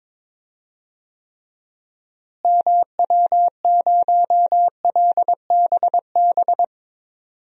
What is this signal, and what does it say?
Morse code, heard as a keyed tone, about 22 wpm, 700 Hz. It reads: MW0LBB
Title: DXCC-100 Call Signs - 6 Characters - Repeated 2x - 22wpm